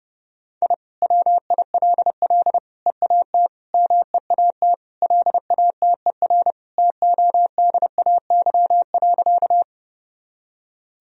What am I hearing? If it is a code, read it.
Morse code, heard as a keyed tone, about 31 wpm, 700 Hz. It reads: I WILL EAT MEAT LATER TODAY.
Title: Sentences from Top 1000 Words - Review Part 3 - 30wpm